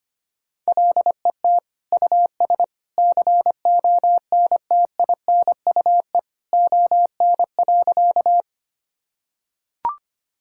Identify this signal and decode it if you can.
Morse code, heard as a keyed tone, about 25 wpm, 700 Hz. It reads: LET US CONTINUE ON.
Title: Sentences from Top 1000 Words - Review Part 6 - 25wpm